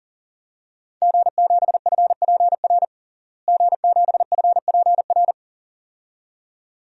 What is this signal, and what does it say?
Morse code, heard as a keyed tone, about 40 wpm, 690 Hz. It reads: G7FPR G7FPR
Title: DXCC-100 Call Signs - 5 Characters - Repeated 2x - 40wpm